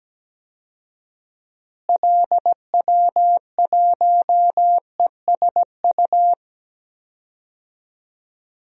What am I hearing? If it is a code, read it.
Morse code, heard as a keyed tone, about 17 wpm, 700 Hz. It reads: LW1ESU
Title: DXCC-100 Call Signs - 6 Characters - Repeated 3x - 17wpm